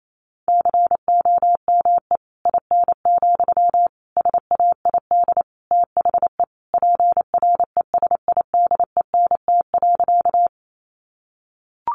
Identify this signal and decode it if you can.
Morse code, heard as a keyed tone, about 28 wpm, 700 Hz. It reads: COME IN, SAID THE PRESIDENT.
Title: Sentences from Top 1000 Words 28wpm